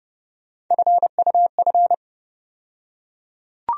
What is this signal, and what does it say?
Morse code, heard as a keyed tone, about 30 wpm, 700 Hz. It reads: FUF
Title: USA Call Sign Postfixes - 3 Letters 30wpm